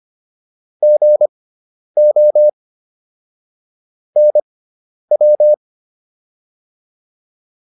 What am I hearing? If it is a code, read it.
Morse code, heard as a keyed tone, about 25 wpm, spaced as for 10 wpm, 600 Hz. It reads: GO NW